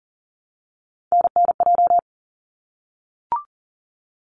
Morse code, heard as a keyed tone, about 39 wpm, 700 Hz. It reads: NNJ
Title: ARRL Sections 40wpm